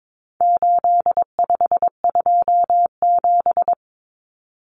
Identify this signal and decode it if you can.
Morse code, heard as a keyed tone, about 22 wpm, 700 Hz. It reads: 8527
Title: Common Cut Numbers - 4-Digit 22wpm